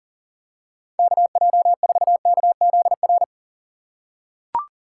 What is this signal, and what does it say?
Morse code, heard as a keyed tone, about 40 wpm, 700 Hz. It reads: KJ4KZR
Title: USA Call Signs - 6 Characters - 3x Character Spacing - 40wpm